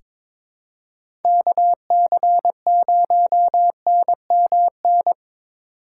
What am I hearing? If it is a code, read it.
Morse code, heard as a keyed tone, about 22 wpm, 700 Hz. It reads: KC0NMN
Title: USA Call Signs - 3x Character Spacing - 22wpm